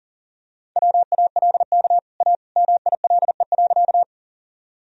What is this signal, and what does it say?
Morse code, heard as a keyed tone, about 40 wpm, 700 Hz. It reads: WALK A MILE.
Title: Sentences from Top 1000 Words - Review Part 1 - 40wpm